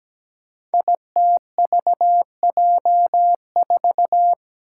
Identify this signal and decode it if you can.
Morse code, heard as a keyed tone, about 17 wpm, 700 Hz. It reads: ITVJ4
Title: Five Challenging Letter-Number Combinations 17wpm